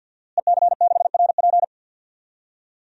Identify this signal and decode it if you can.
Morse code, heard as a keyed tone, about 53 wpm, 700 Hz. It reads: EC6RP
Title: DXCC-100 Call Signs - 5 Characters - Repeated 3x - 50wpm